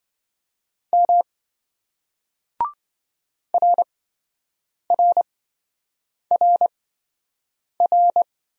Morse code, heard as a keyed tone, about 26 wpm, spaced as for 7 wpm, 700 Hz. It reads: M RRRR